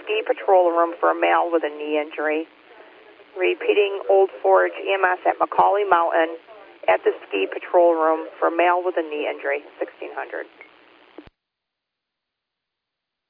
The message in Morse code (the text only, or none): none